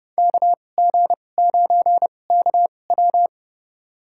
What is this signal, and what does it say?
Morse code, heard as a keyed tone, about 30 wpm, spaced as for 22 wpm, 700 Hz. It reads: KG9KW
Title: USA Call Signs - 5 Characters - 2x Character Spacing - 30wpm